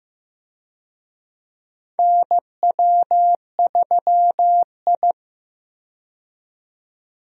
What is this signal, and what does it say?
Morse code, heard as a keyed tone, about 15 wpm, 700 Hz. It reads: NW3I
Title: USA Call Signs - 4 Characters - 15wpm